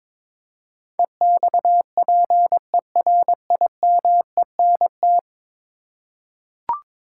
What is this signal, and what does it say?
Morse code, heard as a keyed tone, about 22 wpm, 700 Hz. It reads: EXPERIMENT
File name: practice / 22wpm / 10 Letter Words - First 3 Letters Given - 22wpm